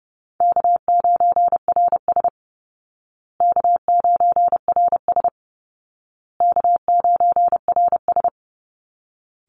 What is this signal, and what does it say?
Morse code, heard as a keyed tone, about 30 wpm, 700 Hz. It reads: K9RS K9RS K9RS
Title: Easy Call Signs Repeated 3x - 30wpm